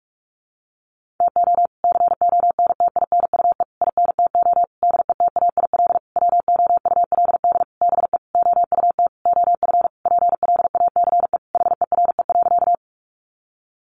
Morse code, heard as a keyed tone, about 44 wpm, 700 Hz. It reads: TO CONTINUE INTO DETAIL WOULD BE OUT OF PLACE HERE.
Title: Sentences from Top 1500 Words 45wpm